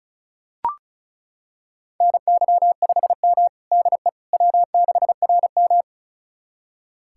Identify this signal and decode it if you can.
Morse code, heard as a keyed tone, about 36 wpm, 700 Hz. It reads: NY5M DE W6RM
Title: USA Callsign DE Callsign - 4 Characters - 35wpm